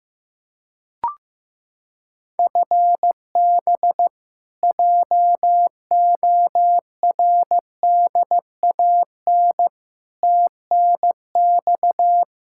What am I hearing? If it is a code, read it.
Morse code, heard as a keyed tone, about 16 wpm, 700 Hz. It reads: FB JORDAN TNX
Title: Practice QSO - Exchange 3 - More Info - 15wpm